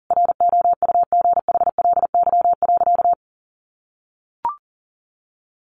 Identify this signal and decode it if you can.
Morse code, heard as a keyed tone, about 40 wpm, 700 Hz. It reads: ROUGHLY.
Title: Sentences from Top 1500 Words 40wpm